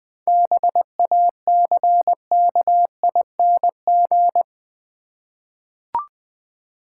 Morse code, heard as a keyed tone, about 21 wpm, 700 Hz. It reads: BACKING
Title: 7 Letter Words - First 3 Letters Given - 20wpm